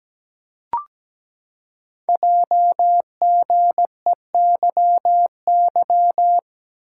Morse code, heard as a keyed tone, about 17 wpm, 700 Hz. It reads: JGEYY
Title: Code Groups - 5 Letters - 17wpm